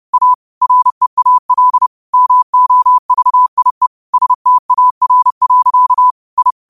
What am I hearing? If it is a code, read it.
Morse code, heard as a keyed tone, about 30 wpm, 1000 Hz. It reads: A REAL MOVIE STAR. I
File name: qotdfiles / 30wpm